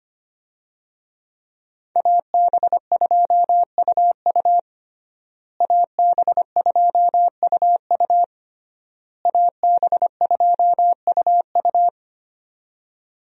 Morse code, heard as a keyed tone, about 25 wpm, 700 Hz. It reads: AB2UU AB2UU AB2UU
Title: USA Call Signs - 5 Characters - Repeated 3x - 25wpm